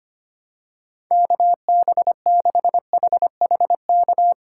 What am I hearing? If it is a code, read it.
Morse code, heard as a keyed tone, about 25 wpm, 700 Hz. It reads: KB6HHK
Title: USA Call Signs - 6 Characters - 6x Character Spacing - 25wpm